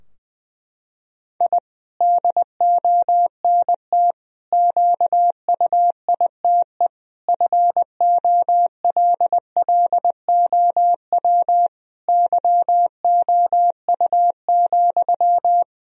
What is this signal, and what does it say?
Morse code, heard as a keyed tone, about 21 wpm, 700 Hz. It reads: I DONT QUITE FOLLOW YOU,
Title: Sentences from Top 1000 Words - Review Part 4 - 20wpm